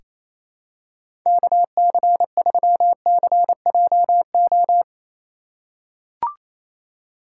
Morse code, heard as a keyed tone, about 28 wpm, 700 Hz. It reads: KC3CJO
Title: USA Call Signs - 2x Character Spacing - 28wpm